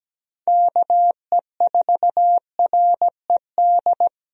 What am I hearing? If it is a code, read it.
Morse code, heard as a keyed tone, about 17 wpm, 700 Hz. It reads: KE4RED
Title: USA Call Signs - 6 Characters - Repeated 3x - 17wpm